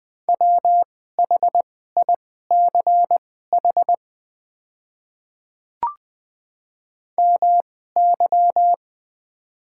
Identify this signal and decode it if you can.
Morse code, heard as a keyed tone, about 20 wpm, spaced as for 14 wpm, 700 Hz. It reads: WHICH MY
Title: 2x Character Spacing - Top 100 Words - 20wpm